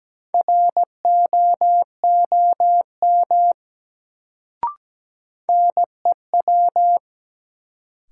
Morse code, heard as a keyed tone, about 17 wpm, 700 Hz. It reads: ROOM NEW
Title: Sentences from Top 300 Words - New Word Drill - 17wpm